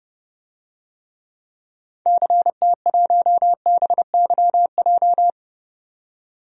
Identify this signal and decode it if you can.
Morse code, heard as a keyed tone, about 30 wpm, 700 Hz. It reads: CT1BYJ
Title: DXCC-100 Call Signs - 6 Characters - 30wpm